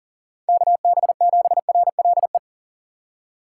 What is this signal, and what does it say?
Morse code, heard as a keyed tone, about 40 wpm, 700 Hz. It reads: KB7RLE
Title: USA Call Signs - 6 Characters - 4x Character Spacing - 40wpm